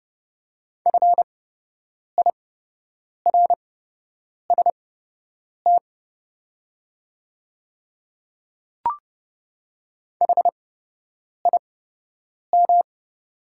Morse code, heard as a keyed tone, about 30 wpm, spaced as for 8 wpm, 700 Hz. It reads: FIRST HIM
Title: 8x Character Spacing - Top 200 Words - 30wpm